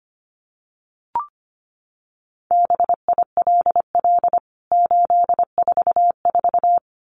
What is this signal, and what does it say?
Morse code, heard as a keyed tone, about 25 wpm, 700 Hz. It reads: BILL 844
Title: CWT Exchange 25wpm